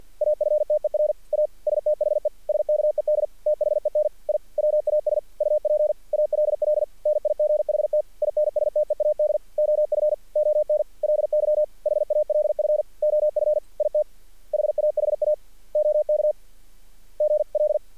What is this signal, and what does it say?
Morse code, heard as a keyed tone, about 46 wpm, 600 Hz. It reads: AFTER A STHE SKED THEN I WAS UP ALL NIGHT INSTEAD OF ONLY HALF OF IT HAHA OK GL
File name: archived / CRSnet